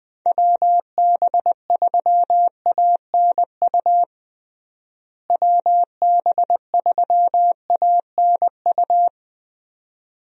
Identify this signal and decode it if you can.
Morse code, heard as a keyed tone, about 21 wpm, 700 Hz. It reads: WB3ANU WB3ANU
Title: USA Call Signs - 6 Characters - Repeated 2x - 20wpm